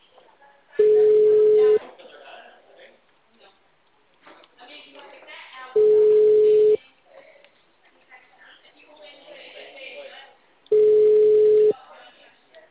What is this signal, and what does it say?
Ringback tone (Europe, 425 Hz), with speech.